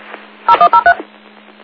DTMF, heard as telephone touch tones, tones of about 90 ms. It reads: 0103